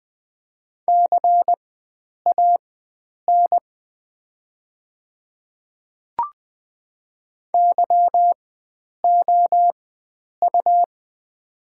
Morse code, heard as a keyed tone, about 20 wpm, spaced as for 9 wpm, 700 Hz. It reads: CAN YOU
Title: 4x Character Spacing - Top 100 Words - 20wpm